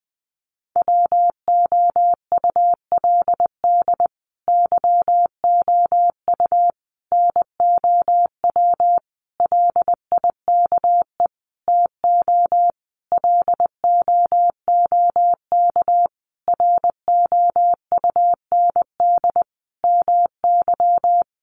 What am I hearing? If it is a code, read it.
Morse code, heard as a keyed tone, about 19 wpm, 700 Hz. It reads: WOULD YOU NOW LIKE TO LOOK ROUND MY
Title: Sentences from Top 200 Words 20wpm